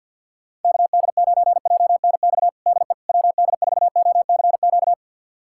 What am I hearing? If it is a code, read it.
Morse code, heard as a keyed tone, about 50 wpm, 700 Hz. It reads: KD9JNX DE WD4OCQ
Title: USA Callsign DE Callsign 50wpm